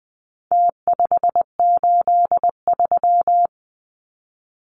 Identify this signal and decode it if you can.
Morse code, heard as a keyed tone, about 20 wpm, 700 Hz. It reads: T583
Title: Numbers - Contest Style - 4-Digit 20wpm